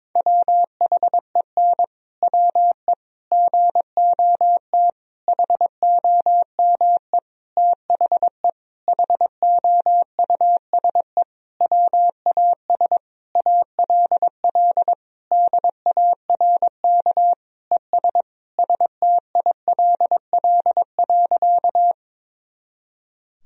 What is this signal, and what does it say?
Morse code, heard as a keyed tone, about 22 wpm, 700 Hz. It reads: WHEN WE GOT HOME THE HOUSE WAS ALL DARK ES STILL.
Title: Sentences from Top 500 Words and Abbreviations 22wpm